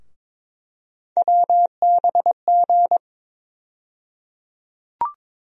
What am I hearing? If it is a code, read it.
Morse code, heard as a keyed tone, about 22 wpm, 700 Hz. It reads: WBG
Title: USA Call Sign Postfixes - 3 Letters 22wpm